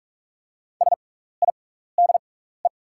Morse code, heard as a keyed tone, about 44 wpm, spaced as for 15 wpm, 690 Hz. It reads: SIDE
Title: 6x Character Spacing - Top 100 Words - 45wpm